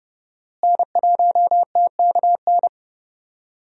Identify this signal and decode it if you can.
Morse code, heard as a keyed tone, about 32 wpm, 700 Hz. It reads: N1TKN
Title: USA Call Signs - 5 Characters - Repeated 2x - 30wpm